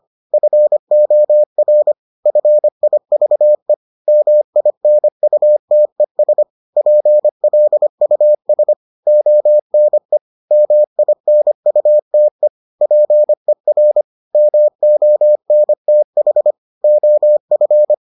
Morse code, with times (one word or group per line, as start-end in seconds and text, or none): 0.3-1.9 FOR
2.3-3.7 FIVE
4.1-6.4 MINUTES
6.8-8.7 PLUS
9.1-10.2 ONE
10.5-12.5 MINUTE
12.8-14.0 PER
14.3-16.5 MONTH
16.8-17.9 OF